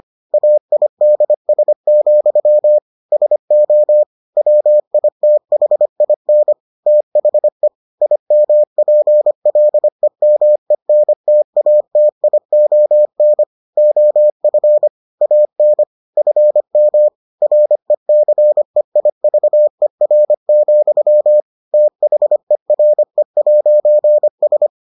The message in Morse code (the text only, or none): AIDS, SO WITHIN THE IMPLEMENTATION OF AN FM RECEIVER, THERE'S